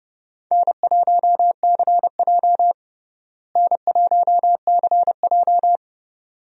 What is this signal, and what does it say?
Morse code, heard as a keyed tone, about 32 wpm, 700 Hz. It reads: N1CJ N1CJ
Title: USA Call Signs - 4 Characters - Repeated 2x - 30wpm